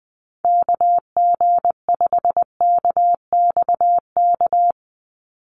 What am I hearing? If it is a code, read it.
Morse code, heard as a keyed tone, about 20 wpm, 700 Hz. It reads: KG5KXK
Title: USA Call Signs 20wpm